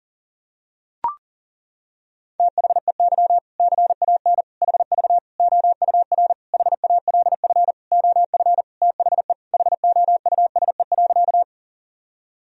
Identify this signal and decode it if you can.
Morse code, heard as a keyed tone, about 40 wpm, 700 Hz. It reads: THEY CAN HV OUR HALF OF THE HOUSE.
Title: Sentences from Top 400 Words and Abbreviations 40wpm